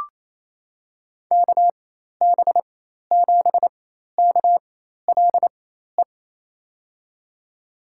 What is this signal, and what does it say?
Morse code, heard as a keyed tone, about 28 wpm, spaced as for 13 wpm, 700 Hz. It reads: KB7KLE
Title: USA Call Signs - 6 Characters - 4x Character Spacing - 28wpm